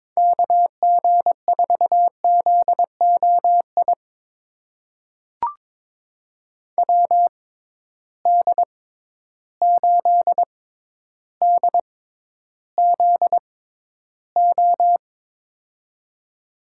Morse code, heard as a keyed tone, about 21 wpm, spaced as for 11 wpm, 700 Hz. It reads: KG4ZOI WD8DZO
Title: USA Call Signs - 6 Characters - 6x Character Spacing - 22wpm